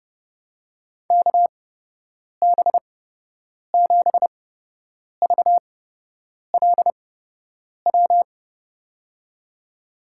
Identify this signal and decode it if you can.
Morse code, heard as a keyed tone, about 30 wpm, spaced as for 8 wpm, 700 Hz. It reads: KB7VLW
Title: USA Call Signs - 6 Characters - 8x Character Spacing - 30wpm